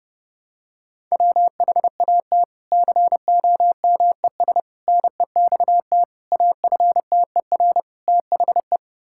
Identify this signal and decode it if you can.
Morse code, heard as a keyed tone, about 28 wpm, 700 Hz. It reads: WHAT COMES NEXT AFTER THE
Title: Sentences from Top 1000 Words - Review Part 5 - 30wpm